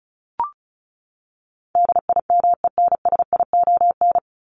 Morse code, heard as a keyed tone, about 34 wpm, 700 Hz. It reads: DIMENSION
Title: Top 1501-1750 Words 35wpm